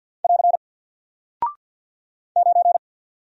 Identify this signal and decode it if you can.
Morse code, heard as a keyed tone, about 48 wpm, 700 Hz. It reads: <AR> 9
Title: Single Character (Letter:Number:Punctuation:Prosign) Alt 50wpm